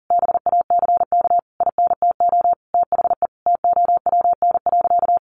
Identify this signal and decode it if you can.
Morse code, heard as a keyed tone, about 40 wpm, 700 Hz. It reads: BACK INTO THE TOWN.
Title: Sentences from Top 1000 Words 40wpm